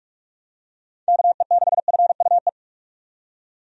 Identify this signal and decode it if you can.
Morse code, heard as a keyed tone, about 46 wpm, 700 Hz. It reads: KE6FUE
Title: USA Call Signs - 6 Characters - 45wpm